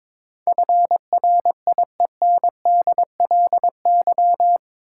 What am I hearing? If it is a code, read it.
Morse code, heard as a keyed tone, about 22 wpm, 700 Hz. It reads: FRIENDLY
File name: practice / 22wpm / Eight Letter Words - 22wpm